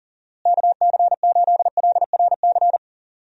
Morse code, heard as a keyed tone, about 40 wpm, 700 Hz. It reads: KC8LRC
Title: USA Call Signs - 6 Characters - Repeated 3x - 40wpm